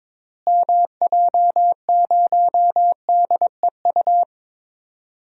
Morse code, heard as a keyed tone, about 22 wpm, 700 Hz. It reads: MJ0DEU
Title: DXCC-100 Call Signs - 6 Characters - Repeated 2x - 22wpm